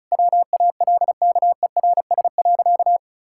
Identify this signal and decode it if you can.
Morse code, heard as a keyed tone, about 36 wpm, 700 Hz. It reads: WALKERS.
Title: Pangram Challenge v2 - 35wpm